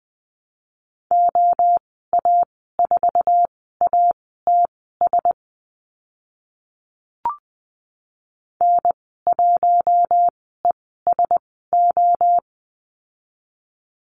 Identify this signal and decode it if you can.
Morse code, heard as a keyed tone, about 20 wpm, spaced as for 14 wpm, 700 Hz. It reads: OA4ATS N1ESO
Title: 069 Lesson 12g - Letter d - Call Signs 20wpm